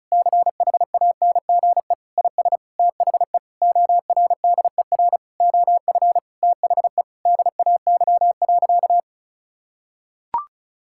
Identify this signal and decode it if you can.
Morse code, heard as a keyed tone, about 34 wpm, 700 Hz. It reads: CHANGE IS THE ORDER OF THE DAY.
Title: Sentences from Top 1000 Words - Review Part 5 - 35wpm